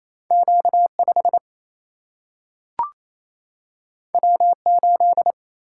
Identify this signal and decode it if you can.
Morse code, heard as a keyed tone, about 28 wpm, 700 Hz. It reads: Q 5 W 8
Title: Single-Letter Single-Number 28wpm